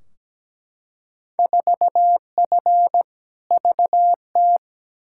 Morse code, heard as a keyed tone, about 17 wpm, 700 Hz. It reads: 4F VT
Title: Field Day Exchanges Encore - 17wpm